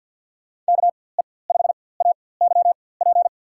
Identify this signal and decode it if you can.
Morse code, heard as a keyed tone, about 53 wpm, spaced as for 24 wpm, 700 Hz. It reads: KE5AYP
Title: USA Call Signs - 4x Character Spacing - 50wpm